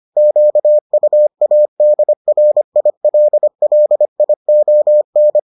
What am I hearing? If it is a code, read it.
Morse code, heard as a keyed tone, about 25 wpm, 600 Hz. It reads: QUADRILLION